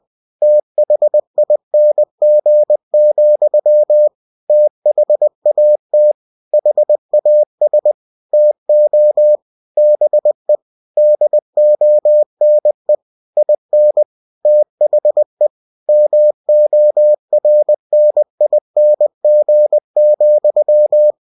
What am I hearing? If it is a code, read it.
Morse code, heard as a keyed tone, about 20 wpm, 600 Hz. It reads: THING, THAT HAS TO BE DONE IN THE MORNING,
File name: cw-books / winnie-the-pooh / winnie-20wpm-page-0024